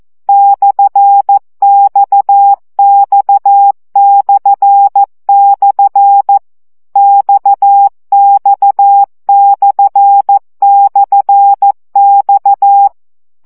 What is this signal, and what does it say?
Morse code, heard as a keyed tone, about 14 wpm, 800 Hz. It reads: /XX// XX//X
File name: CW gyakorlas x per 12 baud